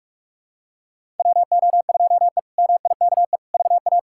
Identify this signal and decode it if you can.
Morse code, heard as a keyed tone, about 46 wpm, 700 Hz. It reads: WO2E MIKE VA
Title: K1USN SST QSO Practice - 45wpm